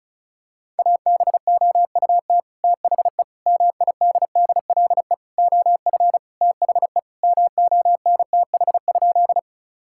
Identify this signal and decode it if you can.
Morse code, heard as a keyed tone, about 36 wpm, 700 Hz. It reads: ABOUT THE MIDDLE OF THE MONTH?
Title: Sentences from Top 1000 Words - Review Part 6 - 35wpm